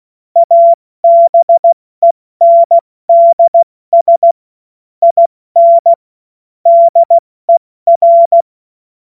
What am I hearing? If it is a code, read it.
Morse code, heard as a keyed tone, about 16 wpm, spaced as for 14 wpm, 680 Hz. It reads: ABENDS IN DER
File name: Morsekurs / Witze / Witz 05. April 2022 70 F80